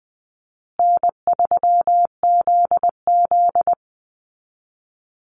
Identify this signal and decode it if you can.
Morse code, heard as a keyed tone, about 19 wpm, 700 Hz. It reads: N3ZZ
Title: Easy Call Signs Repeated 3x - 20wpm